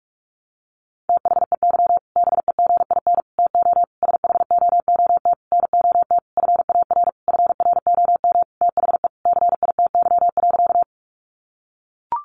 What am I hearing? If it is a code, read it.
Morse code, heard as a keyed tone, about 44 wpm, 700 Hz. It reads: THEY BEGIN TO SHOOT NOT FAR FROM THE CITY.
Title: Sentences from Top 1200 Words 45wpm